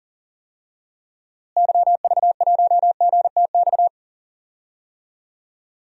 Morse code, heard as a keyed tone, about 40 wpm, 700 Hz. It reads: YV1GTX
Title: DXCC-100 Call Signs - 6 Characters - Repeated 2x - 40wpm